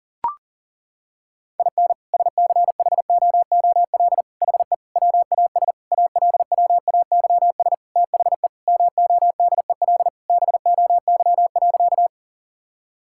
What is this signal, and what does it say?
Morse code, heard as a keyed tone, about 40 wpm, 700 Hz. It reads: IN SCHOOL HE WAS ALWAYS THE MODEL BOY.
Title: Sentences from Top 1000 Words - Review Part 5 - 40wpm